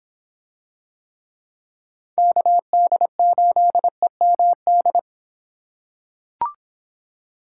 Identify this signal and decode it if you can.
Morse code, heard as a keyed tone, about 26 wpm, 700 Hz. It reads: KD8EMD